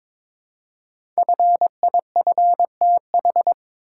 Morse code, heard as a keyed tone, about 22 wpm, 700 Hz. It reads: FIFTH
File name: 5 Letter Words - First 3 Letters Given - 22wpm